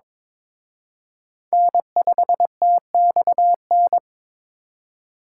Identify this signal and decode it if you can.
Morse code, heard as a keyed tone, about 26 wpm, 700 Hz. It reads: N5TXN
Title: Speed Racing - USA Call Signs 22wpm